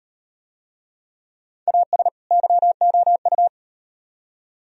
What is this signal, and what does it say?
Morse code, heard as a keyed tone, about 38 wpm, 700 Hz. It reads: AS YOU